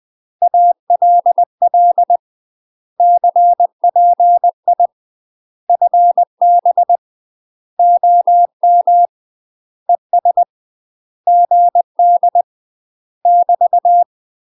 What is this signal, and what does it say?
Morse code, heard as a keyed tone, about 20 wpm, 700 Hz. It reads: ALL CPI FB OM ES GD <BT>
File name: UK MASTER QSOs TRAINING / 20WPM / UK MASTER QSOs 0018